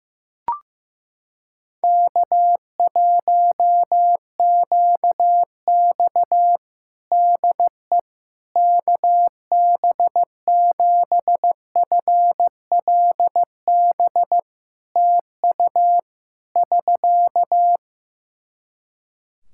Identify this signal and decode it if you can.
Morse code, heard as a keyed tone, about 15 wpm, 700 Hz. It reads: K1QX DE KB7FLB TU <SK>
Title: Practice QSO - Exchange 4 - Ending - 15wpm